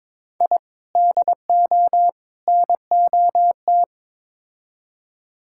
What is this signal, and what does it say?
Morse code, heard as a keyed tone, about 22 wpm, 700 Hz. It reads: I DO NOT
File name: practice / 22wpm / Build A Phrase - 5-Word Phrases - 22wpm